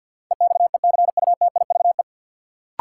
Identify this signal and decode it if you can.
Morse code, heard as a keyed tone, about 52 wpm, 700 Hz. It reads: EXECUTIVE
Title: Sets of 5 Words Encore - New Word Drill - 50wpm